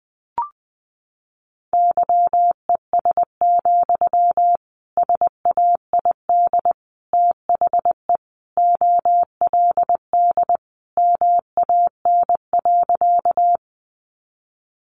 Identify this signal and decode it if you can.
Morse code, heard as a keyed tone, about 20 wpm, 700 Hz. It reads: YES, SAID THE OLD MAN.
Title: Sentences from Top 500 Words 20wpm